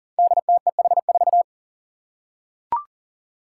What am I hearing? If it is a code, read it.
Morse code, heard as a keyed tone, about 37 wpm, 700 Hz. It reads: DTEH4
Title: Five Challenging Letter-Number Combinations 40wpm